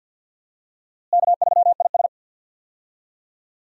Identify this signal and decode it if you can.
Morse code, heard as a keyed tone, about 46 wpm, 700 Hz. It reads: K3IS
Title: USA Call Signs - 4 Characters - Repeated 2x - 50wpm